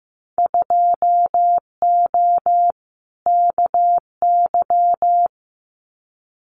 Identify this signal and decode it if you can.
Morse code, heard as a keyed tone, about 15 wpm, 700 Hz. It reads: 2O KY
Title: Winter Field Day Exchanges 15wpm